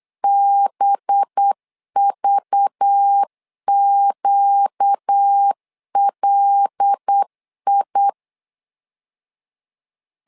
Morse code, heard as a keyed tone, about 8 wpm, 800 Hz. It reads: BVQLI
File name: LEZIONI / 5 LBVC / 5 LBV 8wpm 2837